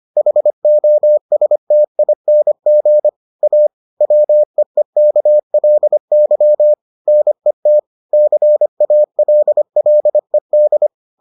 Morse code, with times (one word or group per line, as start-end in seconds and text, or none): 0.2-3.1 HOSTING
3.4-3.7 A
4.0-6.7 WEEKLY
7.1-7.8 NET
8.1-10.9 CALLED